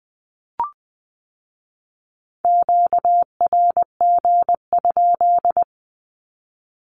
Morse code, heard as a keyed tone, about 20 wpm, 700 Hz. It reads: QRG?
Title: Q-Codes with Spoken Meaning - 20wpm